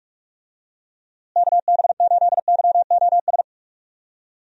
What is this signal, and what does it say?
Morse code, heard as a keyed tone, about 46 wpm, 700 Hz. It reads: KB8YOS
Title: USA Call Signs - 6 Characters - Repeated 2x - 45wpm